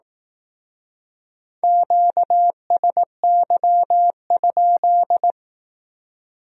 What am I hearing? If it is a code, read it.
Morse code, heard as a keyed tone, about 18 wpm, 700 Hz. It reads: QSY?
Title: Speed-Racing Q-Codes - 15wpm